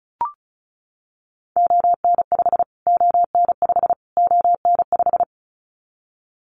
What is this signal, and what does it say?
Morse code, heard as a keyed tone, about 34 wpm, 700 Hz. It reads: ON5 ON5 ON5
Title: Callsign Prefixes for Top 25 DXCC - Repeated 3x - 35wpm